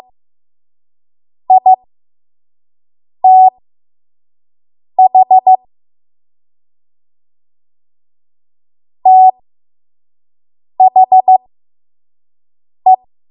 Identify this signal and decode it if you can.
Morse code, heard as a keyed tone, about 15 wpm, spaced as for 5 wpm, 750 Hz. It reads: ITH THE
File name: Morse / 05WPM